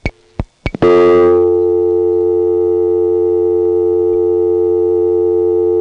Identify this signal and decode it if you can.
Dial tone (North America), 350 + 440 Hz.
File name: phonetrips / HarrisonbgVA1HQ